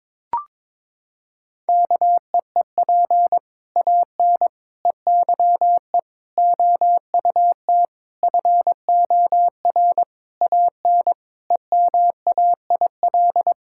Morse code, heard as a keyed tone, about 22 wpm, 700 Hz. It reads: KEEP AN EYE OUT FOR AN EMAIL
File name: Sets of 7 Words - 22wpm